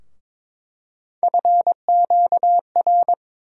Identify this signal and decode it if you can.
Morse code, heard as a keyed tone, about 21 wpm, 700 Hz. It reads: FQR
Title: USA Call Sign Postfixes - 3 Letters 22wpm